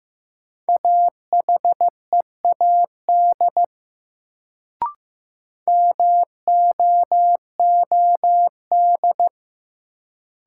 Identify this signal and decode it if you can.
Morse code, heard as a keyed tone, about 15 wpm, 700 Hz. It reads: AHEAD MOOD
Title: Sentences from Top 1100 Words - New Word Drill Alt - 15wpm